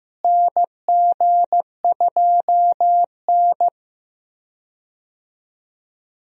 Morse code, heard as a keyed tone, about 16 wpm, 700 Hz. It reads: NG2N
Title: USA Call Signs - 4 Characters - Repeated 2x - 15wpm